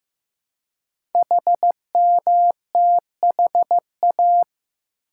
Morse code, heard as a keyed tone, about 15 wpm, 700 Hz. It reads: HMTHA
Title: Code Groups - 5 Letters - 15wpm